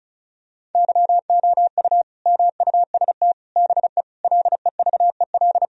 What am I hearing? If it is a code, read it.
Morse code, heard as a keyed tone, about 34 wpm, 700 Hz. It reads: YOU MUST BE LEVEL
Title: Sentences from Top 1000 Words - Review Part 4 - 35wpm